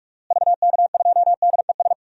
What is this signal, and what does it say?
Morse code, heard as a keyed tone, about 46 wpm, 700 Hz. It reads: VK2DES
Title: DXCC-100 Call Signs - 6 Characters - Repeated 2x - 45wpm